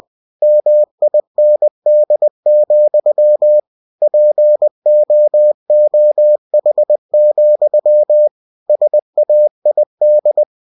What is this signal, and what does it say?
Morse code, heard as a keyed tone, about 20 wpm, 600 Hz. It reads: MIND, POOH, SAID
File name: cw-books / winnie-the-pooh / winnie-20wpm-page-0061